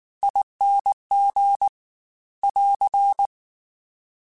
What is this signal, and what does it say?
Morse code, heard as a keyed tone, about 19 wpm, 800 Hz. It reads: ING <AR>
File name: Special INGs / Special INGs-mp3s / Slow ing @18